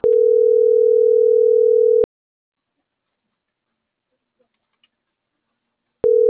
Ringback tone (North America), 440 + 480 Hz.